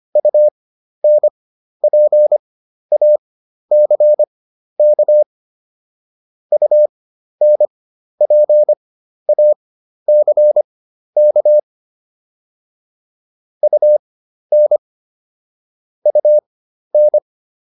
Morse code, heard as a keyed tone, about 25 wpm, spaced as for 12 wpm, 600 Hz. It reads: UNPACK UNPACK UN UN